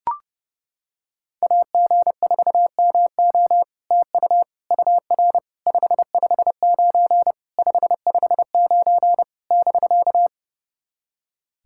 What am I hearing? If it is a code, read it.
Morse code, heard as a keyed tone, about 28 wpm, 690 Hz. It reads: AG4MO TU UR 559 559 <BK>
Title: SOTA QSO Practice 30wpm